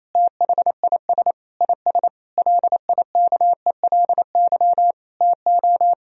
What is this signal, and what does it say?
Morse code, heard as a keyed tone, about 28 wpm, 700 Hz. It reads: THIS IS LIKELY TO
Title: Sentences from Top 1000 Words - Review Part 4 - 28wpm